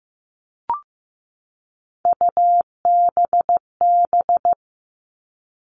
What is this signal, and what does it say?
Morse code, heard as a keyed tone, about 15 wpm, 700 Hz. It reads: UBB
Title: All Cut Numbers - 3-Digit 15wpm